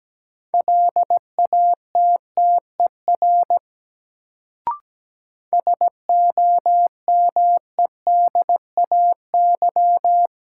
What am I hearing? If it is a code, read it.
Morse code, heard as a keyed tone, about 17 wpm, 700 Hz. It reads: LATTER SOMEDAY
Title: Sentences from Top 1600 Words - New Word Drill - 17wpm